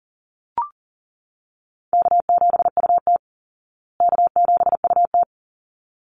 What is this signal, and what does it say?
Morse code, heard as a keyed tone, about 40 wpm, 700 Hz. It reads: K7UT K7UT
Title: Easy Call Signs Repeated 2x - 40wpm